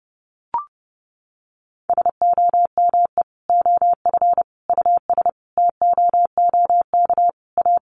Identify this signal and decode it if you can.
Morse code, heard as a keyed tone, about 30 wpm, 700 Hz. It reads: SOME OF US TOOK A
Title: Sentences from Top 1400 Words 30wpm